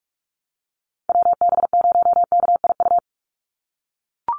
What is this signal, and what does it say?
Morse code, heard as a keyed tone, about 44 wpm, 700 Hz. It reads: WB0KIU
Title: NA SOTA Chaser Callsigns 45wpm